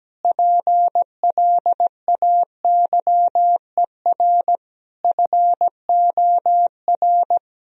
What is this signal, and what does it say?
Morse code, heard as a keyed tone, about 17 wpm, 700 Hz. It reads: PLAYER FOR
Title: Sentences from Top 1000 Words - Review Part 4 - 17wpm